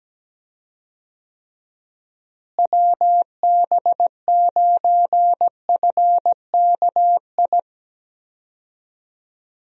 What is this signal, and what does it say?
Morse code, heard as a keyed tone, about 17 wpm, 700 Hz. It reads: WB9FKI
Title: USA Call Signs - 6 Characters - Repeated 3x - 17wpm